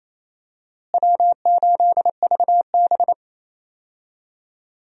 Morse code, heard as a keyed tone, about 28 wpm, 700 Hz. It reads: W8VB
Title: USA Call Signs - 4 Characters - Repeated 3x - 28wpm